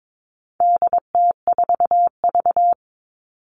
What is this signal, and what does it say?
Morse code, heard as a keyed tone, about 22 wpm, 700 Hz. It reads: DT4V
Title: All Cut Numbers - 4-Digit 22wpm